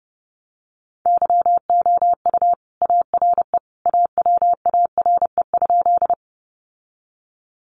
Morse code, heard as a keyed tone, about 30 wpm, 700 Hz. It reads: YOU ARE AWARE?
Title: Sentences from Top 1200 Words 30wpm